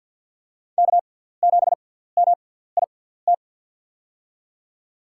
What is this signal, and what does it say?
Morse code, heard as a keyed tone, about 51 wpm, spaced as for 17 wpm, 700 Hz. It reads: K7MIT
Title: USA Call Signs - 5 Characters - 6x Character Spacing - 50wpm